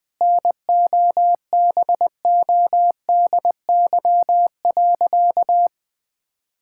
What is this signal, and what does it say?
Morse code, heard as a keyed tone, about 20 wpm, 700 Hz. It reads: NOBODY.
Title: Sentences from Top 1600 Words 20wpm